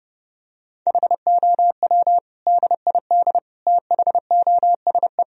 Morse code, heard as a keyed tone, about 30 wpm, 700 Hz. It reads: HOW DID THOSE
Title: Sentences from Top 1000 Words - Review Part 1 - 30wpm